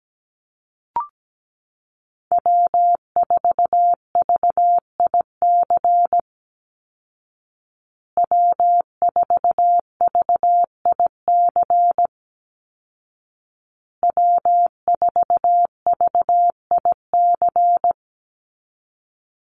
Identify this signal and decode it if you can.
Morse code, heard as a keyed tone, about 17 wpm, 700 Hz. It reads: W4VIC W4VIC W4VIC
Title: Call Signs Repeated 2x - 17wpm